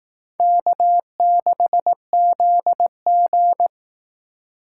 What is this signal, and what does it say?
Morse code, heard as a keyed tone, about 18 wpm, 700 Hz. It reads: K6ZG
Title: Speed Racing - USA 1x2 Call Signs 15wpm